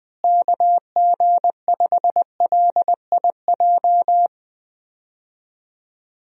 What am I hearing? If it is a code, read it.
Morse code, heard as a keyed tone, about 20 wpm, 700 Hz. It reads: KG5LIJ
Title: USA Call Signs - 6 Characters - Repeated 3x - 20wpm